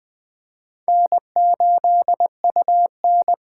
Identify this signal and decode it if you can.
Morse code, heard as a keyed tone, about 24 wpm, 700 Hz. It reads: N8UN
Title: Speed Racing - USA 1x2 Call Signs 20wpm